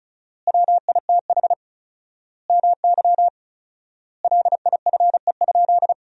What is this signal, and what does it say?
Morse code, heard as a keyed tone, about 36 wpm, 700 Hz. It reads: WITH MY LIFE?
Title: 4x Word-Spacing - Sentences from Top 300 Words - 35wpm